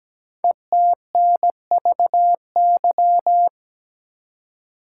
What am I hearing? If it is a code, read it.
Morse code, heard as a keyed tone, about 17 wpm, 700 Hz. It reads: ETNVY
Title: Code Groups - 5 Letters - 17wpm